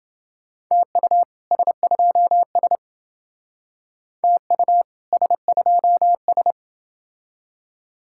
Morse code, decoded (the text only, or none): TU S2S TU S2S